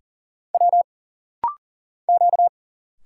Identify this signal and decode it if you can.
Morse code, heard as a keyed tone, about 40 wpm, 700 Hz. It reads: W Q
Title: Single Letters - Rapid-Fire 40wpm